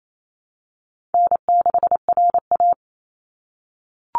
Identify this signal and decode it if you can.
Morse code, heard as a keyed tone, about 28 wpm, 700 Hz. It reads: N6RA
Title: Easy Call Signs - 28wpm